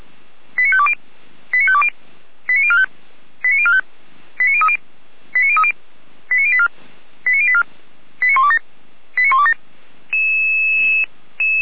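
Five-tone selective call (ZVEI), heard as quick runs of five tones, tones of about 70 ms. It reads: 89520 89520 89045 89045 89030 89030 89084 89084 89127 89127
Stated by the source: ZVEI five-tone selective call 89376 received by SDR.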